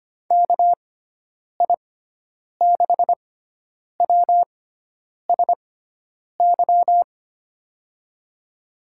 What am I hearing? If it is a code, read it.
Morse code, heard as a keyed tone, about 25 wpm, spaced as for 9 wpm, 700 Hz. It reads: KI6WSY
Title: USA Call Signs - 6 Characters - 6x Character Spacing - 25wpm